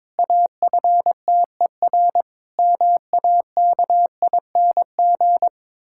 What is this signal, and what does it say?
Morse code, heard as a keyed tone, about 22 wpm, 700 Hz. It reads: AFTER MAKING
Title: Pangram Challenge v2 - 22wpm